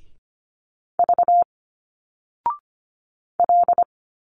Morse code, heard as a keyed tone, about 25 wpm, 700 Hz. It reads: V L
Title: Single Letters 25wpm